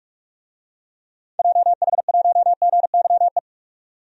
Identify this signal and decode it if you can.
Morse code, heard as a keyed tone, about 43 wpm, 700 Hz. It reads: JH1GYE
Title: DXCC-100 Call Signs - 6 Characters - 45wpm